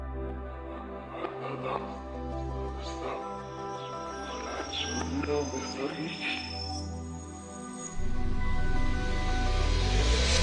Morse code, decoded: none